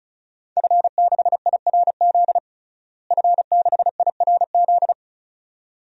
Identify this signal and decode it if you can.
Morse code, heard as a keyed tone, about 34 wpm, 700 Hz. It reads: F6IRZ F6IRZ
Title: DXCC-100 Call Signs - 5 Characters - Repeated 2x - 35wpm